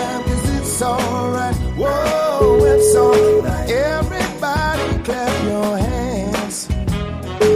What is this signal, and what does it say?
Ringback tone (Europe, 425 Hz), with noise.